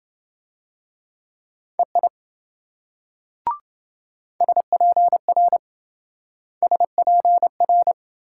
Morse code, heard as a keyed tone, about 28 wpm, 700 Hz. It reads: EI SPR SPR